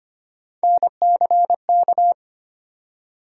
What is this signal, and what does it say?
Morse code, heard as a keyed tone, about 25 wpm, 700 Hz. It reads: NCK
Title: USA Call Sign Postfixes - 3 Letters 25wpm